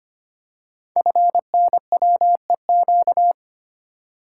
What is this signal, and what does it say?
Morse code, heard as a keyed tone, about 25 wpm, 700 Hz. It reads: FNWEQ
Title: Code Groups - 5 Letters - 25wpm